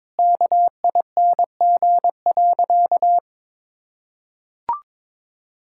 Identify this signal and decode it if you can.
Morse code, heard as a keyed tone, about 22 wpm, 700 Hz. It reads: KING.
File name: practice / 22wpm / Sentences with 4-Word Phrases - 22wpm